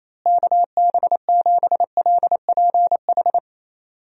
Morse code, heard as a keyed tone, about 28 wpm, 700 Hz. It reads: KB7LPH
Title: USA Call Signs - 6 Characters - Repeated 3x - 28wpm